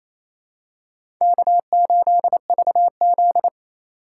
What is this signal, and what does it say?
Morse code, heard as a keyed tone, about 28 wpm, 700 Hz. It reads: K8VZ
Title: USA Call Signs - 8x Character Spacing - 28wpm